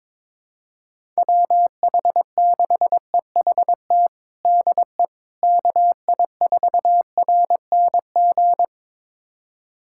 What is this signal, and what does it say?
Morse code, heard as a keyed tone, about 21 wpm, 700 Hz. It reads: WH6EHT DE KI4RNG
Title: USA Callsign DE Callsign 22wpm